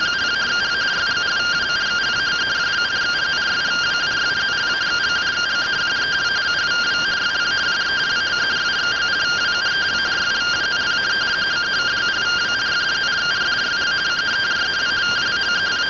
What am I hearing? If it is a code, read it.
Two-tone teleprinter signal, about 170 Hz shift.
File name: findings / CT AXIAL -142.gen-05 rtty rx